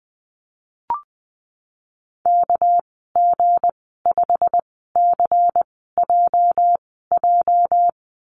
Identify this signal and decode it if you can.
Morse code, heard as a keyed tone, about 20 wpm, spaced as for 14 wpm, 700 Hz. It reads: KG5CJJ
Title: 219 Lesson 35g - Number 0 - Call Signs 20wpm